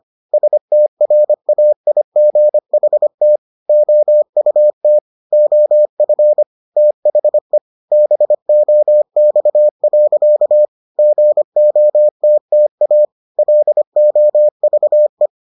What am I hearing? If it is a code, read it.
Morse code, heard as a keyed tone, about 25 wpm, 600 Hz. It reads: STRAIGHT OUT OF THE BOX. GOTTA LOVE